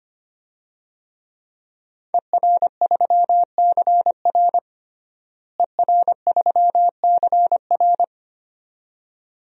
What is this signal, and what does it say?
Morse code, heard as a keyed tone, about 25 wpm, 700 Hz. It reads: ER3CR ER3CR
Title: DXCC-100 Call Signs - 5 Characters - Repeated 2x - 25wpm